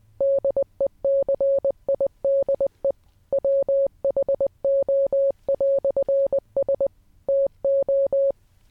Morse code, heard as a keyed tone, about 20 wpm, 560 Hz. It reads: DECIDE WHO"S TO